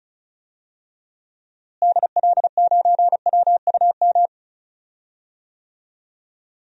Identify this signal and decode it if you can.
Morse code, heard as a keyed tone, about 36 wpm, 700 Hz. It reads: DL9WUM
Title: DXCC-100 Call Signs - 6 Characters - Repeated 2x - 35wpm